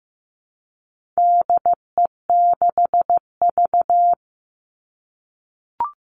Morse code, heard as a keyed tone, about 15 wpm, 700 Hz. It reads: DE6V
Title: All Cut Numbers - 4-Digit 15wpm